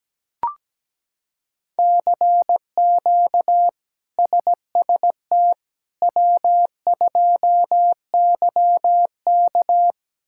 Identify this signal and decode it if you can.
Morse code, heard as a keyed tone, about 17 wpm, 700 Hz. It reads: CQ SST W2YK
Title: K1USN SST QSO Practice - 17wpm